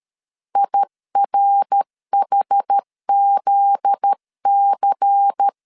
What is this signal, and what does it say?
Morse code, heard as a keyed tone, about 13 wpm, 800 Hz. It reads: IRHZC
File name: LEZIONI / 8 YJ45 / 8 YJ45 12wpm 2151